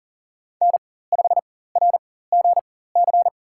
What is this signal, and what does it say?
Morse code, heard as a keyed tone, about 40 wpm, spaced as for 19 wpm, 700 Hz. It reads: N5RGC
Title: USA Call Signs - 5 Characters - 4x Character Spacing - 40wpm